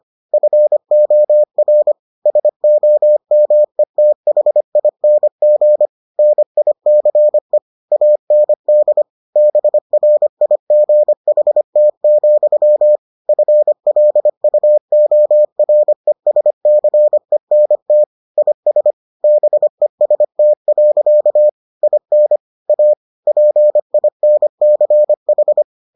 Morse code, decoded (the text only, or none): FOR SOMETHING NICE AND BRIGHT, FLUORESCENT IS BEST. IN A PINCH